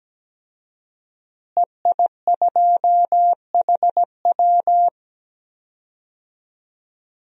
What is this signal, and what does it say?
Morse code, heard as a keyed tone, about 17 wpm, 700 Hz. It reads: EI2HW
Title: DXCC-100 Call Signs - 5 Characters - Repeated 3x - 17wpm